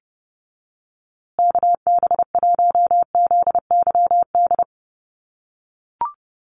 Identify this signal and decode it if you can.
Morse code, heard as a keyed tone, about 29 wpm, 700 Hz. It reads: KB1ZYD
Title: USA Call Signs Repeated 3x 30wpm